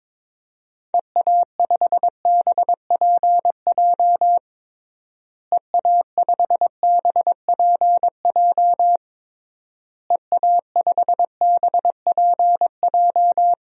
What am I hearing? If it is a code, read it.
Morse code, heard as a keyed tone, about 22 wpm, 700 Hz. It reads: EA5BPJ EA5BPJ EA5BPJ
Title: DXCC-100 Call Signs - 6 Characters - Repeated 3x - 22wpm